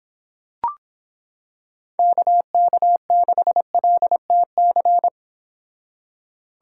Morse code, heard as a keyed tone, about 26 wpm, 700 Hz. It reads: KK6LTC